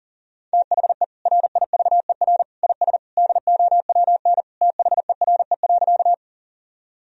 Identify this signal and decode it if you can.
Morse code, heard as a keyed tone, about 40 wpm, 700 Hz. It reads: THE RIVER IS DOWN THERE.
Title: Sentences from Top 1000 Words - Review Part 1 - 40wpm